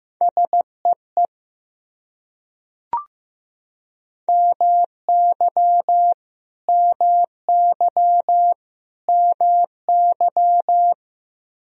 Morse code, heard as a keyed tone, about 15 wpm, 700 Hz. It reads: SEE MY MY MY
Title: Top 100 Words Repeated 15wpm